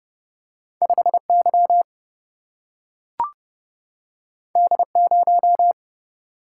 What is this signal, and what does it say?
Morse code, heard as a keyed tone, about 30 wpm, 700 Hz. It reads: 5Y D0
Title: Single-Letter Single-Number 30wpm